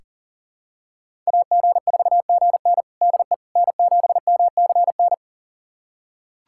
Morse code, heard as a keyed tone, about 39 wpm, 700 Hz. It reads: AG4GN DE N7MCN
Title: USA Callsign DE Callsign - 5 Characters - 40wpm